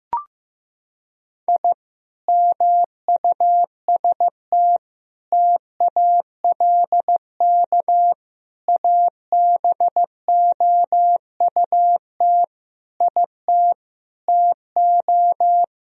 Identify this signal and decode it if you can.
Morse code, heard as a keyed tone, about 15 wpm, 700 Hz. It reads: I MUST TALK ABOUT IT TO
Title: Sentences from Top 1000 Words - Review Part 4 - 15wpm